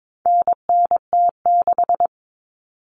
Morse code, heard as a keyed tone, about 22 wpm, 700 Hz. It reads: NNT6
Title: Common Cut Numbers - 4-Digit 22wpm